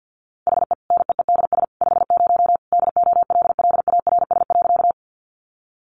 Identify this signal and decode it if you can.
Morse code, heard as a keyed tone, about 50 wpm, 700 Hz. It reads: HE NEEDS 50 DOLLARS.
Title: Additional Easy Phrases 50wpm